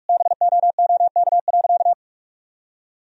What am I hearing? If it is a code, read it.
Morse code, heard as a keyed tone, about 46 wpm, 700 Hz. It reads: BOOK.
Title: Sentences from Top 1000 Words - Review Part 1 - 45wpm